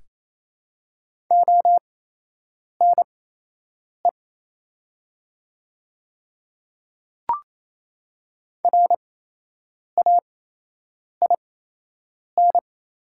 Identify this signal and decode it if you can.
Morse code, heard as a keyed tone, about 28 wpm, spaced as for 8 wpm, 700 Hz. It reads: ONE RAIN
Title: 8x Character Spacing - Top 300 Words - 28wpm